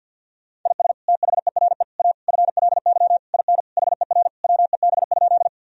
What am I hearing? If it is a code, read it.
Morse code, heard as a keyed tone, about 50 wpm, 700 Hz. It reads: IS THERE A FLY IN HER WEB?
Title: Sentences from Top 1000 Words - Review Part 4 - 50wpm